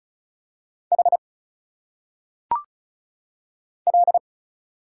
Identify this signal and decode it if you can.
Morse code, heard as a keyed tone, about 34 wpm, 700 Hz. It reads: H L
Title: Single Letters - Phonetic Pronunciation 35wpm